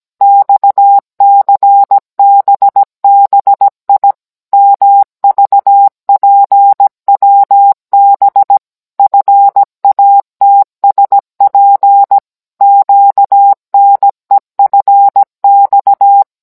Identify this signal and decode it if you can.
Morse code, heard as a keyed tone, about 17 wpm, 800 Hz. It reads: XCBBI MVPWB FATSP QNEFX